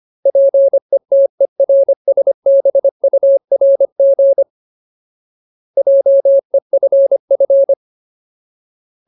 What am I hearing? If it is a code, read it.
Morse code, heard as a keyed tone, about 25 wpm, 550 Hz. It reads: PETERSBURG JEFF
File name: Ultimate Word List / 25WPM Ultimate